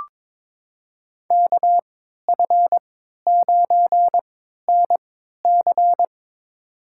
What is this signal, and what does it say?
Morse code, heard as a keyed tone, about 22 wpm, spaced as for 13 wpm, 700 Hz. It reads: KF9NC
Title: USA Call Signs - 5 Characters - 3x Character Spacing - 22wpm